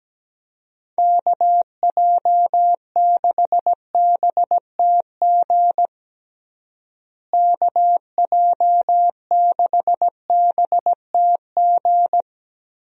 Morse code, heard as a keyed tone, about 17 wpm, 700 Hz. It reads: KJ6BTG KJ6BTG
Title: USA Call Signs - 6 Characters - Repeated 3x - 17wpm